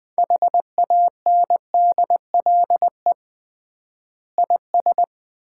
Morse code, heard as a keyed tone, about 20 wpm, 700 Hz. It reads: HANDLE IS
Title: Common QSO Phrases - 3x Word-Spacing - 20wpm